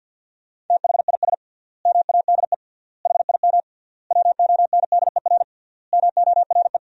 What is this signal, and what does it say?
Morse code, heard as a keyed tone, about 53 wpm, 700 Hz. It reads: THIS MADE HIM WONDER MORE
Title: 3x Word-Spacing - Sentences from Top 500 Words - 50wpm